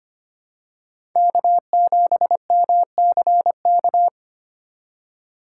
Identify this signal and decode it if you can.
Morse code, heard as a keyed tone, about 25 wpm, 700 Hz. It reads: K7MCK
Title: USA Call Signs - 5 Characters - 4x Character Spacing - 25wpm